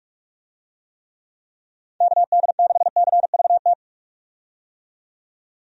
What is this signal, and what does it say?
Morse code, heard as a keyed tone, about 46 wpm, 700 Hz. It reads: KD6CVT
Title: USA Call Signs - 6 Characters - 45wpm